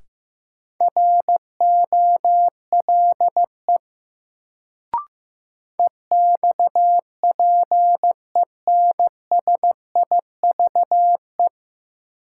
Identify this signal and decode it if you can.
Morse code, heard as a keyed tone, about 15 wpm, 700 Hz. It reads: ROLE EXPENSIVE
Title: Sentences from Top 1500 Words - New Word Drill Alt - 15wpm